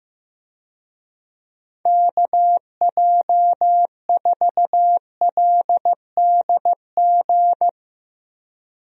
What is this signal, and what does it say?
Morse code, heard as a keyed tone, about 15 wpm, 700 Hz. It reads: KJ4LDG
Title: USA Call Signs - 6 Characters - Repeated 3x - 15wpm